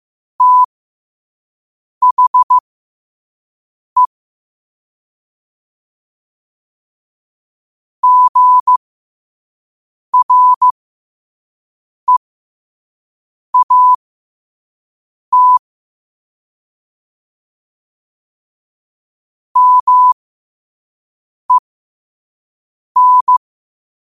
Morse code, heard as a keyed tone, about 15 wpm, spaced as for 5 wpm, 1000 Hz. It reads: THE GREAT MEN
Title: 5 WPM morse code quotes for Fri, 01 Mar 2024